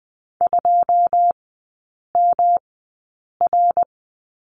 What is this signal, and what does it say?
Morse code, heard as a keyed tone, about 20 wpm, spaced as for 8 wpm, 690 Hz. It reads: 2MR
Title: 145 Lesson 24e - Letter y - Three Characters 20wpm